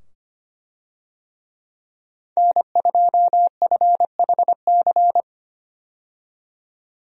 Morse code, heard as a keyed tone, about 25 wpm, 700 Hz. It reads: N2FHC
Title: USA Call Signs - 5 Characters - 25wpm